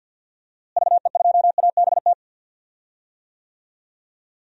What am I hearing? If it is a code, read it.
Morse code, heard as a keyed tone, about 52 wpm, 700 Hz. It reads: VE2ABT
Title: DXCC-100 Call Signs - 6 Characters - Repeated 3x - 50wpm